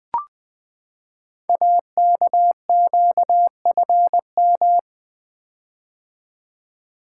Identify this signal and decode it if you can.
Morse code, heard as a keyed tone, about 21 wpm, 700 Hz. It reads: AKQFM
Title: Code Groups - 5 Letter:Numbers - 20wpm